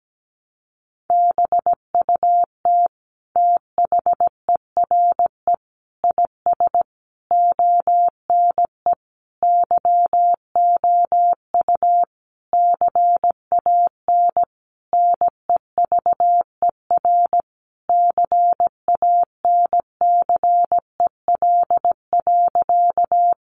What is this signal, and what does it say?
Morse code, heard as a keyed tone, about 17 wpm, 700 Hz. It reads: BUT THERE IS ONE YOU CAN NEVER CANCEL.
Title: Sentences from Top 1200 Words 17wpm